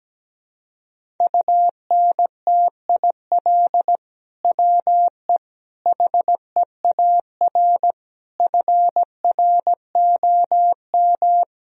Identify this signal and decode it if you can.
Morse code, heard as a keyed tone, about 17 wpm, 700 Hz. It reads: UNTIL WE HEAR FROM
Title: Sentences from Top 1000 Words - Review Part 1 - 17wpm